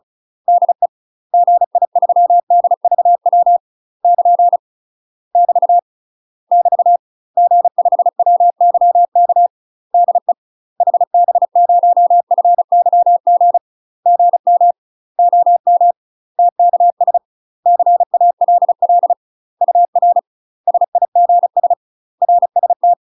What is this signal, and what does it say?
Morse code, heard as a keyed tone, about 36 wpm, 700 Hz. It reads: DE GI3DVW <KN> <BT> <BT> G5WYK DE HB0FYG GM OM TKS CALL UR SIGS RST
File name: UK MASTER QSOs TRAINING / 35WPM / UK MASTER QSOs 0017